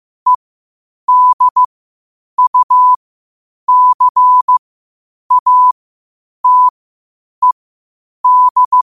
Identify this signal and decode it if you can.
Morse code, heard as a keyed tone, about 15 wpm, spaced as for 8 wpm, 1000 Hz. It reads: EDUCATED
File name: qotdfiles / 8wpm